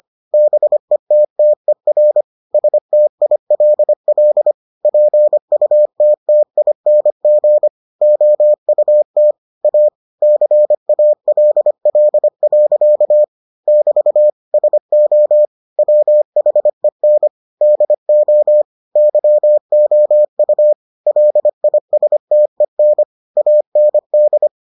Morse code, heard as a keyed tone, about 25 wpm, 600 Hz. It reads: BETTER STILL PUTTING OUT A CALL. <BT> SO WHEN DO YOU LISTEN AND